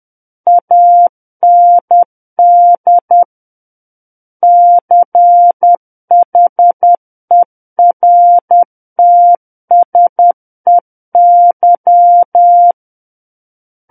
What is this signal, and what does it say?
Morse code, heard as a keyed tone, about 10 wpm, 700 Hz. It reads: AND CHERTSEY
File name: Book One / 01-Chapter 1 - 10 WPM